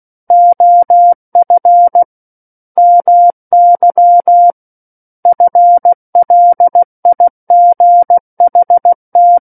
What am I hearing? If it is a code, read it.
Morse code, heard as a keyed tone, about 16 wpm, 700 Hz. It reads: OF MY FLIGHT
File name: Book One / 07-Chapter 7 - 16 WPM